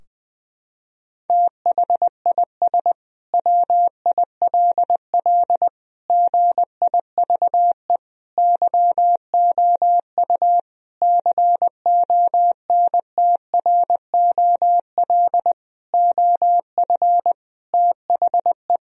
Morse code, heard as a keyed tone, about 20 wpm, 700 Hz. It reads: THIS WILL GIVE YOU CONTROL OF THE
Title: Sentences from Top 1000 Words - Review Part 3 - 20wpm